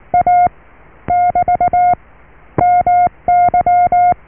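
Morse code, heard as a keyed tone, about 19 wpm, 710 Hz. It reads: A <BT> MY
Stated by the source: Morse QSO on 14 MHz with a German station, received at RST 579.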